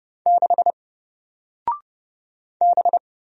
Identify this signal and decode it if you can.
Morse code, heard as a keyed tone, about 30 wpm, 700 Hz. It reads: 6 B
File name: Single-Letter-Number-30wpm